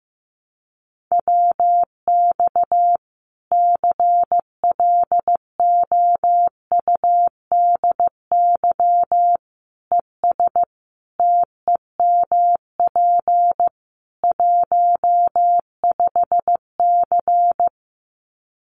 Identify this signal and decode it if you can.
Morse code, heard as a keyed tone, about 15 wpm, 700 Hz. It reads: WX CLOUDY ES TEMP 15C
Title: Common QSO Phrases 15wpm